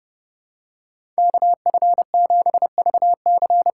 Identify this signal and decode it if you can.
Morse code, heard as a keyed tone, about 31 wpm, 700 Hz. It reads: KF7VC
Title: USA Call Signs - 6 Characters - 6x Character Spacing - 30wpm